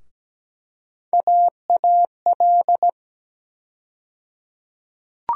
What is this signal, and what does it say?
Morse code, heard as a keyed tone, about 17 wpm, 700 Hz. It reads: AAL
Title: USA Call Sign Postfixes - 3 Letters 17wpm